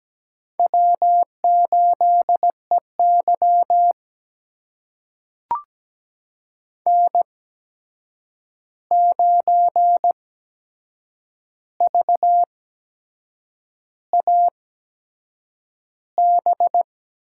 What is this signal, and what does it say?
Morse code, heard as a keyed tone, about 17 wpm, spaced as for 7 wpm, 700 Hz. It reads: W8EY N9VAB
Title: USA Call Signs - 8x Character Spacing - 17wpm